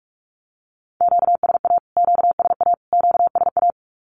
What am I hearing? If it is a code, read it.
Morse code, heard as a keyed tone, about 44 wpm, 700 Hz. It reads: QSA QSA QSA
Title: Q-Codes Repeated 3x 45wpm